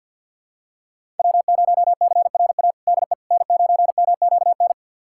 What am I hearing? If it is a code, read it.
Morse code, heard as a keyed tone, about 52 wpm, 700 Hz. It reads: W0CRA DE N9MQN
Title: USA Callsign DE Callsign - 5 Characters - 50wpm